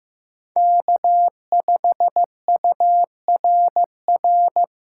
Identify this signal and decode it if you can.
Morse code, heard as a keyed tone, about 15 wpm, 700 Hz. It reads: K5URR
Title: USA Call Signs - 8x Character Spacing - 15wpm